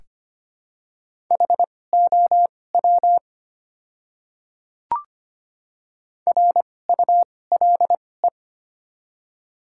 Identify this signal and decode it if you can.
Morse code, heard as a keyed tone, about 25 wpm, spaced as for 18 wpm, 700 Hz. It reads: HOW RULE
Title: 2x Character Spacing - Top 500 Words - 25wpm